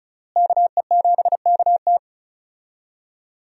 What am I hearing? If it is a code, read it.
Morse code, heard as a keyed tone, about 34 wpm, 700 Hz. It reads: KE7KT
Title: USA Call Signs - 5 Characters - Repeated 3x - 35wpm